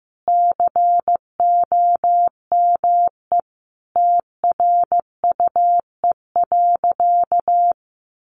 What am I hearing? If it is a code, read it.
Morse code, heard as a keyed tone, about 15 wpm, 700 Hz. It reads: COME TRUE.
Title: Sentences from Top 1500 Words 15wpm